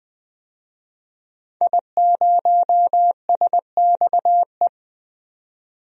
Morse code, heard as a keyed tone, about 21 wpm, 700 Hz. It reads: I0SXE
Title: DXCC-100 Call Signs - 5 Characters - 20wpm